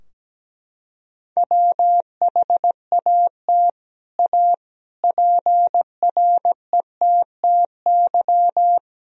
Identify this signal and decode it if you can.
Morse code, heard as a keyed tone, about 17 wpm, 700 Hz. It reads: WHAT A PRETTY
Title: Sentences from Top 1000 Words - Review Part 2 - 17wpm